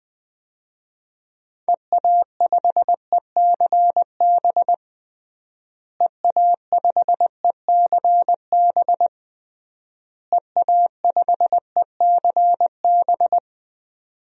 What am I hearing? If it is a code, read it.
Morse code, heard as a keyed tone, about 21 wpm, 700 Hz. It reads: EA5ECB EA5ECB EA5ECB
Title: DXCC-100 Call Signs - 6 Characters - Repeated 3x - 20wpm